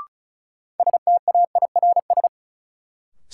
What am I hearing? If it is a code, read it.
Morse code, heard as a keyed tone, about 36 wpm, 700 Hz. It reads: STAIRS
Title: Sets of 5 Words Encore - New Word Drill - 35wpm